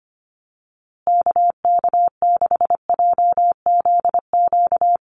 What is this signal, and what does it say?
Morse code, heard as a keyed tone, about 25 wpm, 700 Hz. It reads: KK6JZQ
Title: USA Call Signs Repeated 2x 25wpm